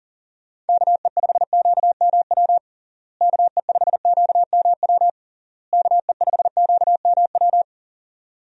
Morse code, heard as a keyed tone, about 40 wpm, 700 Hz. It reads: KE5QMW KE5QMW KE5QMW
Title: USA Call Signs - 6 Characters - Repeated 3x - 40wpm